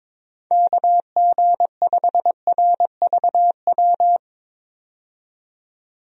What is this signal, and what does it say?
Morse code, heard as a keyed tone, about 22 wpm, 700 Hz. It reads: KG5RVW
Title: USA Call Signs - 6 Characters - 22wpm